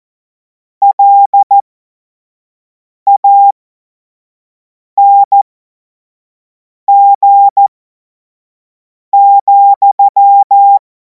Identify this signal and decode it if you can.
Morse code, heard as a keyed tone, about 14 wpm, spaced as for 5 wpm, 800 Hz. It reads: LANG,
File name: Morsekurs / Witze / Witz 10. März 2026 25 F70